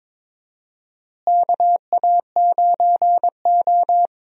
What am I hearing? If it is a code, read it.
Morse code, heard as a keyed tone, about 22 wpm, 700 Hz. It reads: KA9O
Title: USA Call Signs - 4 Characters - Repeated 2x - 22wpm